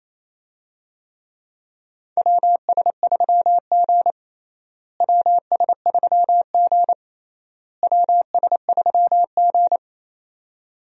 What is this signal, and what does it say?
Morse code, heard as a keyed tone, about 28 wpm, 700 Hz. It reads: WS3G WS3G WS3G
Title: USA Call Signs - 4 Characters - Repeated 3x - 28wpm